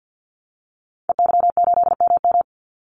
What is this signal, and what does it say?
Morse code, heard as a keyed tone, about 50 wpm, 700 Hz. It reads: EY8MM
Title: Call Signs Repeated 2x - 50wpm